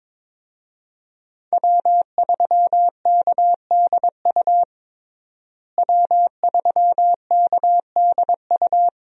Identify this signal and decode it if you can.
Morse code, heard as a keyed tone, about 22 wpm, 700 Hz. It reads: W3KDU W3KDU
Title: USA Call Signs - 5 Characters - Repeated 3x - 22wpm